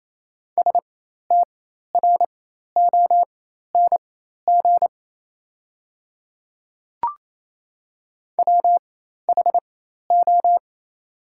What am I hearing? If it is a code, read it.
Morse code, heard as a keyed tone, about 28 wpm, spaced as for 13 wpm, 700 Hz. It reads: STRONG WHO
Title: 4x Character Spacing - Top 500 Words - 28wpm